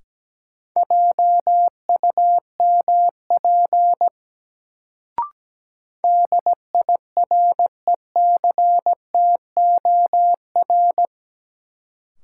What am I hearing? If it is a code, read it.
Morse code, heard as a keyed tone, about 17 wpm, 700 Hz. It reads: JUMP DIRECTOR
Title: Sentences from Top 1300 Words - New Word Drill - 17wpm